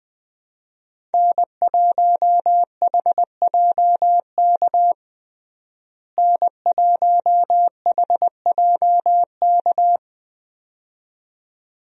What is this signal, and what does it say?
Morse code, heard as a keyed tone, about 20 wpm, 700 Hz. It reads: N1HJK N1HJK
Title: USA Call Signs - 5 Characters - Repeated 2x - 20wpm